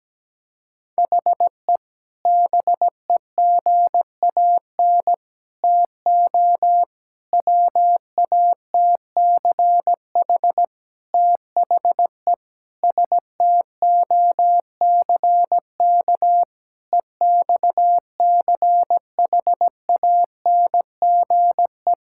Morse code, heard as a keyed tone, about 17 wpm, 700 Hz. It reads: HE BEGAN TO WATCH THE STOCK EXCHANGE
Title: Sentences from Top 1000 Words - Review Part 6 - 17wpm